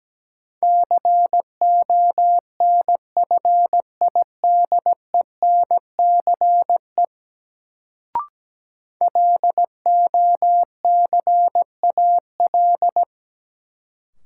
Morse code, heard as a keyed tone, about 17 wpm, 700 Hz. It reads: CONFIDENCE LOCAL
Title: Sentences from Top 1500 Words - New Word Drill Alt - 17wpm